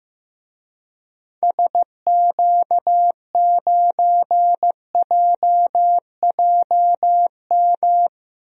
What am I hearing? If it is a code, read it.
Morse code, heard as a keyed tone, about 15 wpm, 700 Hz. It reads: SQ9JJM
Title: DXCC-100 Call Signs - 6 Characters - 15wpm